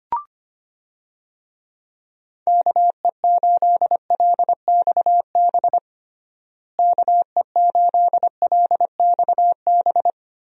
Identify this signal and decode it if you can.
Morse code, heard as a keyed tone, about 25 wpm, 700 Hz. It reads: KE8LXB KE8LXB
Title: USA Call Signs - 6 Characters - Repeated 2x - 25wpm